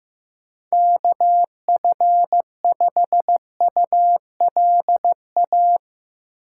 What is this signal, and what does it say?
Morse code, heard as a keyed tone, about 15 wpm, 700 Hz. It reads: KF5ULA
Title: USA Call Signs - 6 Characters - 4x Character Spacing - 15wpm